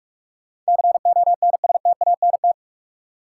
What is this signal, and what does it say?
Morse code, heard as a keyed tone, about 47 wpm, 700 Hz. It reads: CONSTANT
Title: Sentences from Top 1500 Words - New Word Drill - 45wpm